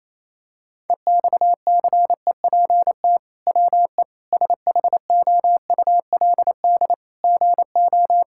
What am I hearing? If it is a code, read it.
Morse code, heard as a keyed tone, about 28 wpm, 700 Hz. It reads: EXCEPT WE SHOULD GO
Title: Sentences from Top 1600 Words 28wpm